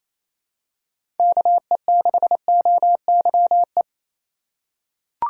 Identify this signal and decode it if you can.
Morse code, heard as a keyed tone, about 28 wpm, 700 Hz. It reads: KE6OYE
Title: USA Call Signs - 4x Character Spacing - 28wpm